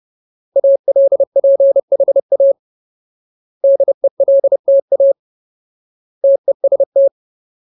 Morse code, heard as a keyed tone, about 30 wpm, 550 Hz. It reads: ALPHA DELTA TEST
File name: Ultimate Word List / 30WPM Ultimate